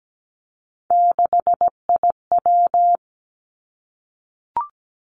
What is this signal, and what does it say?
Morse code, heard as a keyed tone, about 17 wpm, 700 Hz. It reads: 6IW
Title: Three Challenging Letter-Number Combinations 17wpm